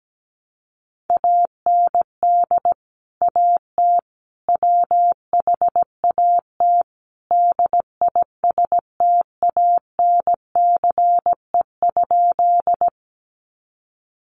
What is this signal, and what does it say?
Morse code, heard as a keyed tone, about 17 wpm, 700 Hz. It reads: AND AT WHAT DISTANCE?
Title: Sentences from Top 1200 Words 17wpm